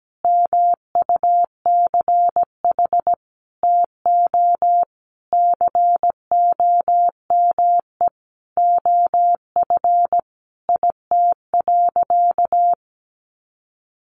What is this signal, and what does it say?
Morse code, heard as a keyed tone, about 17 wpm, 700 Hz. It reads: MUCH TO COME OF IT.
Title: Sentences from Top 1100 Words 17wpm